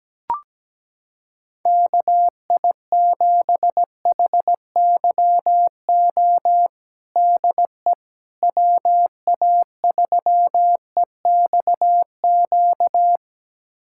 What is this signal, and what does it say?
Morse code, heard as a keyed tone, about 17 wpm, 700 Hz. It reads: KI7HYO DE WA3EXQ
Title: USA Callsign DE Callsign - 6 Characters - 17wpm